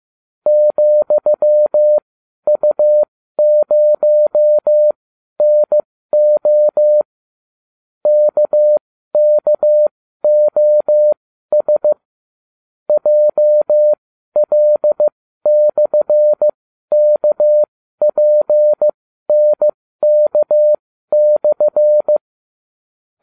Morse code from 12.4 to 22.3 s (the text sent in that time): JL/KPNK/